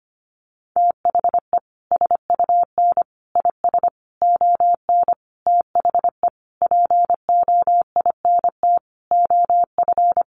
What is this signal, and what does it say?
Morse code, heard as a keyed tone, about 25 wpm, 700 Hz. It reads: THE SUN IS ON THE POINT OF
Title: Sentences from Top 1300 Words 25wpm